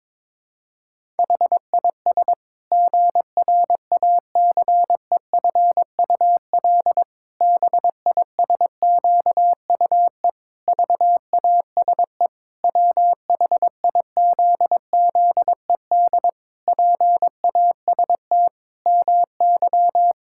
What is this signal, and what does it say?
Morse code, heard as a keyed tone, about 22 wpm, 700 Hz. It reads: HIS GRACEFUL BISQUE VASE WHIZZED PAST MY
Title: Pangram Challenge v2 - 22wpm